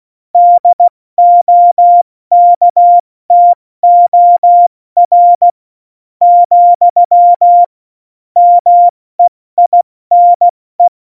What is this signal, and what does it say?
Morse code, heard as a keyed tone, about 16 wpm, 700 Hz. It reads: DOKTOR , MEINE
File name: Morsekurs / Witze / Witz 06 March 2018, 70 F80